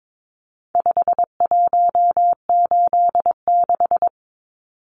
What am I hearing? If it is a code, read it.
Morse code, heard as a keyed tone, about 22 wpm, 700 Hz. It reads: 5186
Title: Numbers - 4-Digit 22wpm